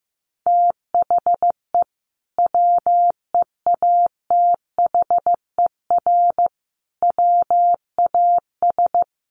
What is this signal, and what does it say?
Morse code, heard as a keyed tone, about 15 wpm, 700 Hz. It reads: THE WEATHER WAS
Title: Sentences from Top 1400 Words 15wpm